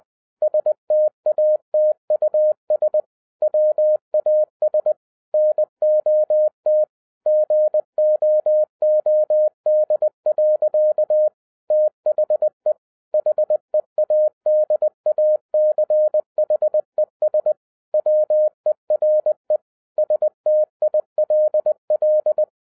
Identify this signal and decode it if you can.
Morse code, heard as a keyed tone, about 20 wpm, 610 Hz. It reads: STATUS WAS NOT GOOD. THE HEADACHES WERE STILL
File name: 2011-07-mp3 / n58 BACK FROM A BIG ACCIDENT 20wpm